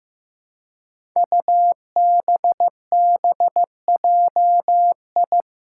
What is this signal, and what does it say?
Morse code, heard as a keyed tone, about 15 wpm, 700 Hz. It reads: UBBJI
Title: Five Challenging Letter-Number Combinations 15wpm